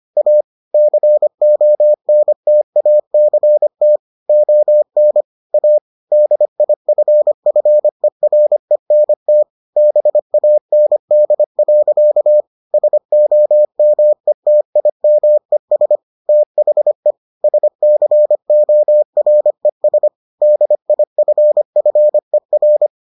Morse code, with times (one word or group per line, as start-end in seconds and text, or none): 0.2-0.4 A
0.7-4.0 CONTACT
4.3-5.2 ON
5.5-5.8 A
6.1-9.4 DIFFERENT
9.8-12.4 BAND.
12.7-16.0 SOMETIMES
16.3-17.1 THE
17.4-20.1 SCORES
20.4-22.9 DIFFER